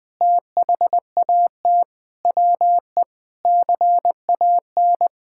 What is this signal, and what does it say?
Morse code, heard as a keyed tone, about 20 wpm, 700 Hz. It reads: THAT WE CAN
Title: Sentences from Top 1000 Words - Review Part 6 - 20wpm